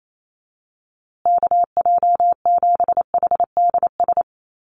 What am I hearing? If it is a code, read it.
Morse code, heard as a keyed tone, about 28 wpm, 700 Hz. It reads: KJ7HDS
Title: USA Call Signs Repeated 3x 28wpm